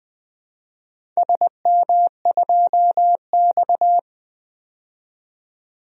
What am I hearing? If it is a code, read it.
Morse code, heard as a keyed tone, about 20 wpm, 700 Hz. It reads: SM2X
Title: DXCC-100 Call Signs - 4 Characters - 20wpm